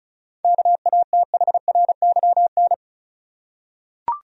Morse code, heard as a keyed tone, about 36 wpm, 700 Hz. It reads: KATHRYN
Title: USA Common Names - 35wpm